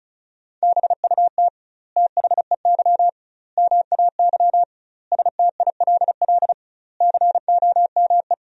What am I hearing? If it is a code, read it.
Morse code, heard as a keyed tone, about 34 wpm, 700 Hz. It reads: BUT THEY MAY STILL COME
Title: 2x Word-Spacing - Sentences from Top 300 Words - 35wpm